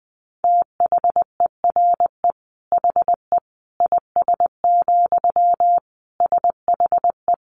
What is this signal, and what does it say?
Morse code, heard as a keyed tone, about 20 wpm, 700 Hz. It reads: THERE HE IS, SHE
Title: Sentences from Top 100 Words 20wpm